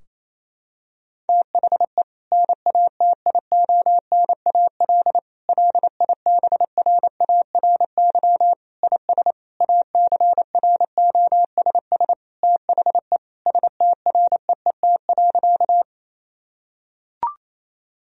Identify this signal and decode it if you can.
Morse code, heard as a keyed tone, about 28 wpm, 700 Hz. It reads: THE NATIONAL LIBRARY IS ACROSS THE STREET.
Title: Sentences from Top 1000 Words - Review Part 4 - 28wpm